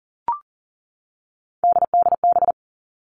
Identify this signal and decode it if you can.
Morse code, heard as a keyed tone, about 40 wpm, 700 Hz. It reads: DDB
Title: All Cut Numbers - 3-Digit 40wpm